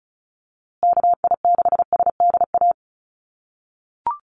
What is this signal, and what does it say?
Morse code, heard as a keyed tone, about 34 wpm, 700 Hz. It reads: KI6SDA
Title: USA Call Signs Repeated 3x 35wpm